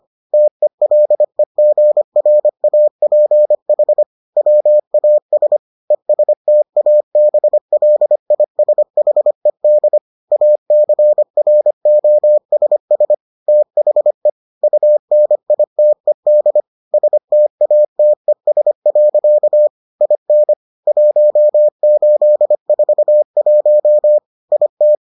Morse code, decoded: TELEGRAPH WAS ESTABLISHED ACROSS THE UNITED STATES. IN 1841 IT

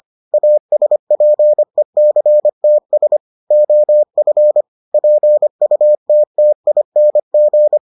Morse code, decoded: ASPECTS OF PUTTING

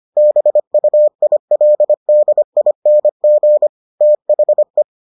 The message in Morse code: BUILDING THE

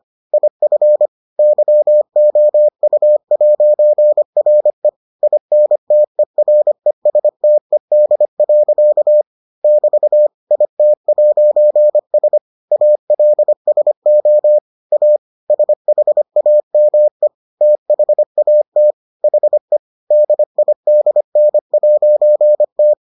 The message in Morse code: IF YOU'RE INTERESTED. <BT> IT'S ALSO A SHAME THAT HE DIDN'T

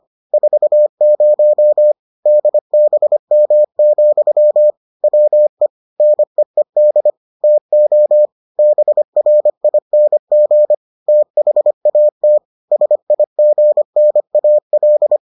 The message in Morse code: 40 DBM, WE NEED TO BRING THAT SIGNAL

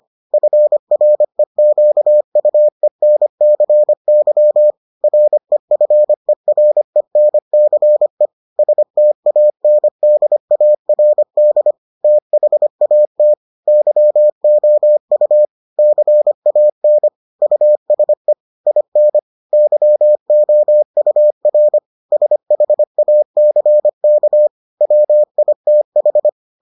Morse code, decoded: FREQUENCY REFERENCE STANDARD THAT YOU CAN USE IN YOUR SHACK WITH

